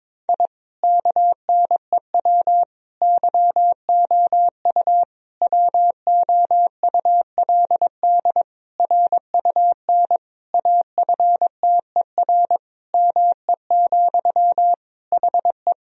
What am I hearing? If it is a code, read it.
Morse code, heard as a keyed tone, about 22 wpm, 700 Hz. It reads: I KNEW YOU WOULD RUN AFTER ME, HE
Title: Sentences from Top 1000 Words - Review Part 1 - 22wpm